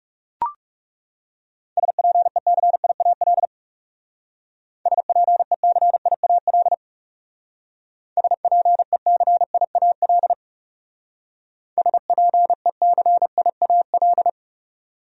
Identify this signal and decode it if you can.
Morse code, heard as a keyed tone, about 37 wpm, 700 Hz. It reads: SPECIAL SPECIAL SPECIAL SPECIAL
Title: Speed-Racing Top 500 Words 30wpm